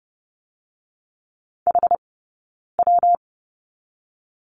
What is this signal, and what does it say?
Morse code, heard as a keyed tone, about 30 wpm, 700 Hz. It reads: H W
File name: Lesson 19d - Letter w - Two Characters 30wpm